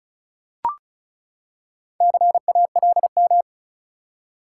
Morse code, heard as a keyed tone, about 36 wpm, 700 Hz. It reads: CALM
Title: Four Letter Words - 35wpm